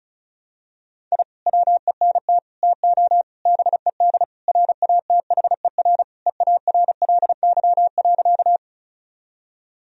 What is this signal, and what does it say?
Morse code, heard as a keyed tone, about 34 wpm, 700 Hz. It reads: I WENT TO BED RATHER EARLY.
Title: Sentences from Top 1000 Words - Review Part 4 - 35wpm